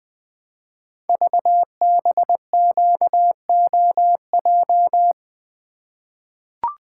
Morse code, heard as a keyed tone, about 21 wpm, 700 Hz. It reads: VBQOJ
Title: Code Groups - 5 Letters - 20wpm